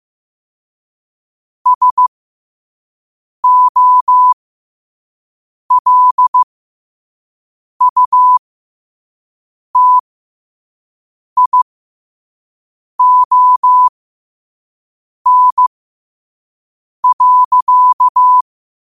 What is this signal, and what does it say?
Morse code, heard as a keyed tone, about 15 wpm, spaced as for 5 wpm, 1000 Hz. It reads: SOLUTION.
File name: qotdfiles / 5wpm